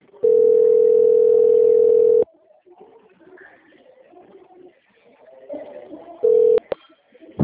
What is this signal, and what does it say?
Ringback tone (North America), 440 + 480 Hz.